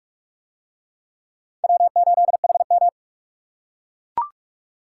Morse code, heard as a keyed tone, about 45 wpm, 700 Hz. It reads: W8HM